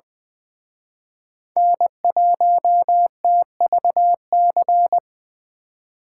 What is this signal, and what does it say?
Morse code, heard as a keyed tone, about 25 wpm, 700 Hz. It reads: N1TVC
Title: Speed Racing - USA 1x3 Call Signs 20wpm